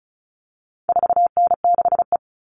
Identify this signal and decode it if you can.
Morse code, heard as a keyed tone, about 34 wpm, 700 Hz. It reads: 4N6E
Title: All Cut Numbers - 4-Digit 35wpm